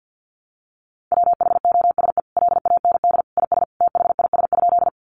Morse code, heard as a keyed tone, about 50 wpm, 700 Hz. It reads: WHOSE LAND IS THIS?
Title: Sentences from Top 1200 Words 50wpm